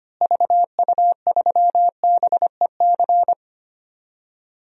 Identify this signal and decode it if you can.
Morse code, heard as a keyed tone, about 25 wpm, 700 Hz. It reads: VU3BEC
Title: DXCC-100 Call Signs - 6 Characters - Repeated 2x - 25wpm